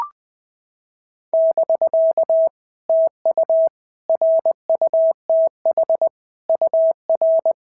Morse code, heard as a keyed tone, about 20 wpm, 650 Hz. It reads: <BK> TU RUTH UR
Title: POTA QSO Practice - 20wpm